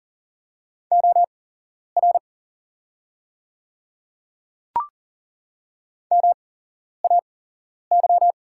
Morse code, heard as a keyed tone, about 40 wpm, spaced as for 11 wpm, 700 Hz. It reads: OR MAY
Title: 8x Character Spacing - Top 100 Words - 40wpm